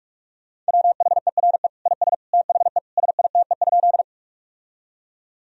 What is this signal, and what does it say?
Morse code, heard as a keyed tone, about 46 wpm, 700 Hz. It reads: WHERE IS THE SITE?
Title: Sentences from Top 1000 Words - Review Part 4 - 45wpm